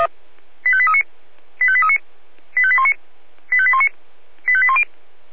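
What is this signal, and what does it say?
Five-tone selective call (ZVEI), heard as quick runs of five tones, tones of about 70 ms. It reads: 85839 85839 85819 85819 85810
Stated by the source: Five-tone ZVEI selective call received on an SDR.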